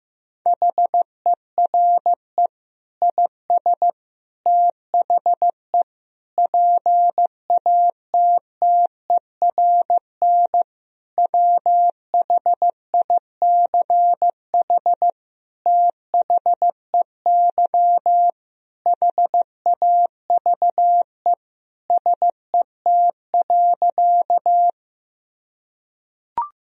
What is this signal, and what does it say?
Morse code, heard as a keyed tone, about 15 wpm, 700 Hz. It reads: HERE IS THE PATTERN WHICH THEY HAVE SET.
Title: Sentences from Top 1000 Words - Review Part 2 - 15wpm